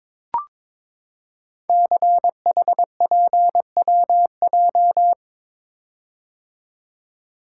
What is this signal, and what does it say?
Morse code, heard as a keyed tone, about 22 wpm, 700 Hz. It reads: CHPWJ
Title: Code Groups - 5 Letters - 22wpm